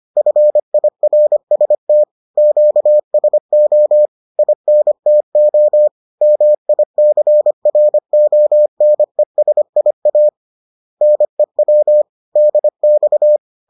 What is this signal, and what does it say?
Morse code, heard as a keyed tone, about 25 wpm, 600 Hz. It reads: FIRST QSO INTO MICRONESIA NEW DX